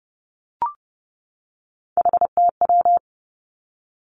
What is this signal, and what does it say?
Morse code, heard as a keyed tone, about 31 wpm, 700 Hz. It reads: HTW
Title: Three Challenging Letter-Number Combinations 30wpm